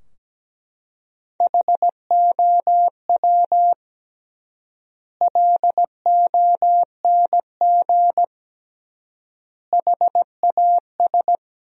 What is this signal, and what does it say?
Morse code, heard as a keyed tone, about 17 wpm, 700 Hz. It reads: HOW LONG HAS
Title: 3x Word-Spacing - Sentences from Top 100 Words - 17wpm